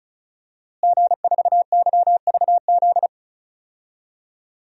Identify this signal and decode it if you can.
Morse code, heard as a keyed tone, about 34 wpm, 700 Hz. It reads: G4YVZ
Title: DXCC-100 Call Signs - 5 Characters - Repeated 2x - 35wpm